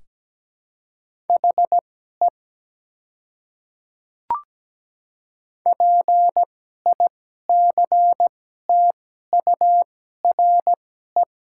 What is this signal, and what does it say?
Morse code, heard as a keyed tone, about 17 wpm, spaced as for 12 wpm, 700 Hz. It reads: HE PICTURE
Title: 2x Character Spacing - Top 200 Words - 17wpm